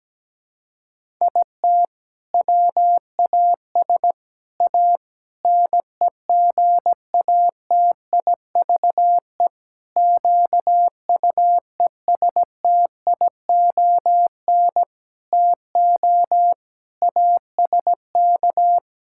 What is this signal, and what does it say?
Morse code, heard as a keyed tone, about 17 wpm, 700 Hz. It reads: IT WAS A NEGATIVE QUESTION TO ASK
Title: Sentences from Top 1000 Words - Review Part 6 - 17wpm